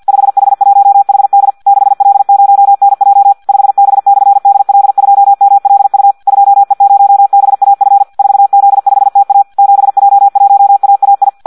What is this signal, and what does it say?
Morse code, heard as a keyed tone, about 50 wpm, 800 Hz. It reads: 5H1HD 6L0NJ 5B/DL2MLU 2E0BAF 4Z5TA 7J1AAI
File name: Audio practica CW / qrq-tx-50 2